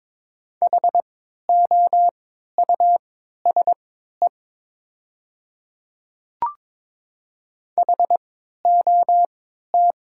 Morse code, heard as a keyed tone, about 22 wpm, spaced as for 13 wpm, 700 Hz. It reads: HOUSE HOT
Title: 3x Character Spacing - Top 300 Words - 22wpm